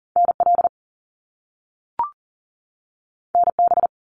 Morse code, heard as a keyed tone, about 40 wpm, 700 Hz. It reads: NL NB
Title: Canadian Provinces and Territories 40wpm